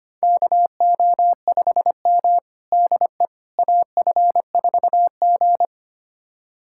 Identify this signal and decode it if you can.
Morse code, heard as a keyed tone, about 25 wpm, 700 Hz. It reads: KO5M DE AF4G
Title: USA Callsign DE Callsign - 4 Characters - 25wpm